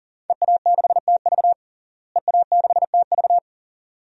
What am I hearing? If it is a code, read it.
Morse code, heard as a keyed tone, about 40 wpm, 700 Hz. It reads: EA6TV EA6TV
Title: DXCC-100 Call Signs - 5 Characters - Repeated 2x - 40wpm